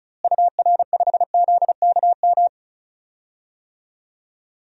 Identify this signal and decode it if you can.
Morse code, heard as a keyed tone, about 36 wpm, 700 Hz. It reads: UR5ZKM
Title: DXCC-100 Call Signs - 6 Characters - Repeated 3x - 35wpm